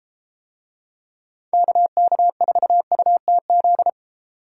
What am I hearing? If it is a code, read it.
Morse code, heard as a keyed tone, about 31 wpm, 700 Hz. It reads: KK4UTZ